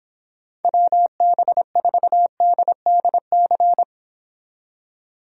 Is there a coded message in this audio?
Yes: Morse code from 0.6 to 3.8 s, heard as a keyed tone, about 29 wpm, 700 Hz. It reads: WB4DDC